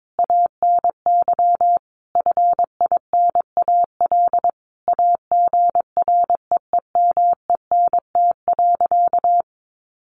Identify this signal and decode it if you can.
Morse code, heard as a keyed tone, about 22 wpm, 700 Hz. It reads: ANY FINAL AGREEMENT.
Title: Sentences from Top 1500 Words 22wpm